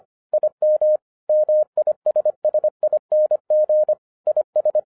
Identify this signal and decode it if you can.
Morse code, heard as a keyed tone, about 25 wpm, 610 Hz. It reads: IM MISSING IS
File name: 2011-10-mp3 / n04-JUST MOVED 25wpm